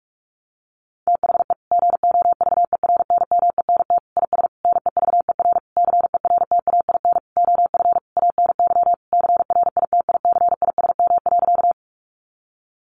Morse code, heard as a keyed tone, about 44 wpm, 700 Hz. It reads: THE GOVERNMENT IS NEVER CERTAIN OF ANY CRITICISM.
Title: Sentences from Top 1300 Words 45wpm